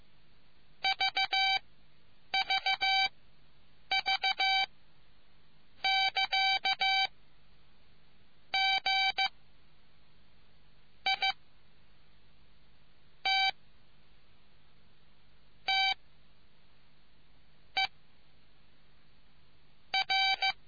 Morse code, heard as a keyed tone, about 15 wpm, spaced as for 8 wpm, 770 Hz. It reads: VVV <KA> G I T T E R